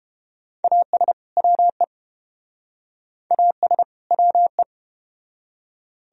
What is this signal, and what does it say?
Morse code, heard as a keyed tone, about 32 wpm, 700 Hz. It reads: AS WE AS WE